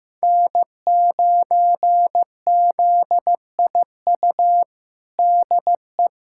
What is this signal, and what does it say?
Morse code, heard as a keyed tone, about 15 wpm, 700 Hz. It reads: N9ZIU DE
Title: USA Callsign DE Callsign 15wpm